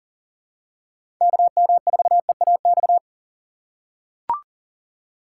Morse code, heard as a keyed tone, about 40 wpm, 700 Hz. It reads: KM4EAX
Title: USA Call Signs - 6 Characters - 3x Character Spacing - 40wpm